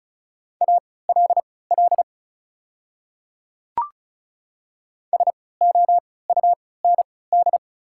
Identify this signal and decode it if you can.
Morse code, heard as a keyed tone, about 34 wpm, spaced as for 20 wpm, 690 Hz. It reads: ALL SOUND
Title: 3x Character Spacing - Top 100 Words - 35wpm